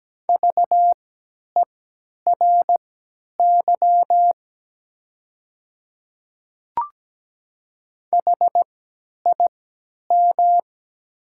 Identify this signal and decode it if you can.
Morse code, heard as a keyed tone, about 17 wpm, spaced as for 10 wpm, 690 Hz. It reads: VERY HIM
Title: 3x Character Spacing - Top 200 Words - 17wpm